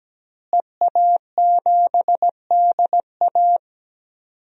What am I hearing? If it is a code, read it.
Morse code, heard as a keyed tone, about 17 wpm, 700 Hz. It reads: EA7DA
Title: DXCC-100 Call Signs - 5 Characters - Repeated 3x - 17wpm